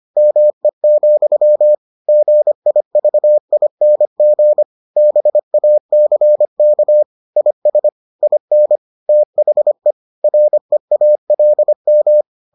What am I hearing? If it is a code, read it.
Morse code, heard as a keyed tone, about 25 wpm, 600 Hz. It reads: ME, GIVING BACK IS IN THE REALM